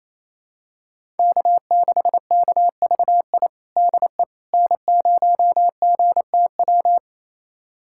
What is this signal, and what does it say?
Morse code, heard as a keyed tone, about 28 wpm, 700 Hz. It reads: K6KVI DE N0GTW
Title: USA Callsign DE Callsign - 5 Characters - 28wpm